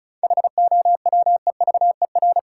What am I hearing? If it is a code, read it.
Morse code, heard as a keyed tone, about 36 wpm, 700 Hz. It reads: HOWEVER
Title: Sentences from Top 600 Words - New Word Drill - 35wpm